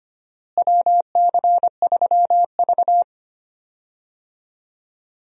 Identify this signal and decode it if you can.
Morse code, heard as a keyed tone, about 25 wpm, 700 Hz. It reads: WC3V
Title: USA Call Signs - 4 Characters - Repeated 3x - 25wpm